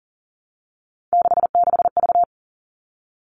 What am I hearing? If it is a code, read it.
Morse code, heard as a keyed tone, about 40 wpm, 700 Hz. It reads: 66V
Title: All Cut Numbers - 3-Digit 40wpm